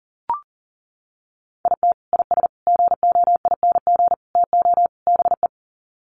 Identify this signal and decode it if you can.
Morse code, heard as a keyed tone, about 40 wpm, 700 Hz. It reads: IT IS GOING TO BE
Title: Sets of 5 Words 40wpm